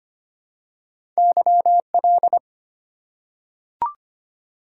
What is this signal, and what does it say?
Morse code, heard as a keyed tone, about 25 wpm, 700 Hz. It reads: YL
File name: Top 90 QSO Elements - 8x Character Spacing - 25wpm